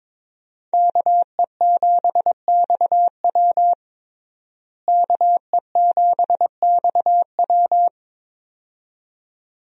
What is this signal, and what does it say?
Morse code, heard as a keyed tone, about 23 wpm, 700 Hz. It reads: KE7XW KE7XW
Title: USA Call Signs - 5 Characters - Repeated 2x - 22wpm